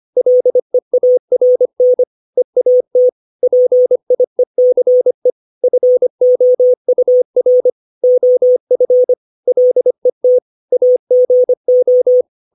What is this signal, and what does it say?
Morse code, heard as a keyed tone, about 25 wpm, 500 Hz. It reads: LEARN EAT PIECE FOUR OF LET AGO
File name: 632 words / 632 Words 25WPM